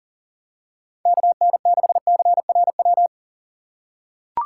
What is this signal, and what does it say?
Morse code, heard as a keyed tone, about 40 wpm, 690 Hz. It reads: KN6CRW
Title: USA Call Signs - 2x Character Spacing - 40wpm